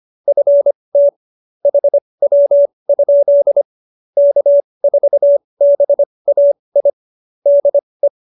Morse code, heard as a keyed tone, about 25 wpm, spaced as for 20 wpm, 570 Hz. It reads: FT HW? K4BAI DE
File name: QSO301 20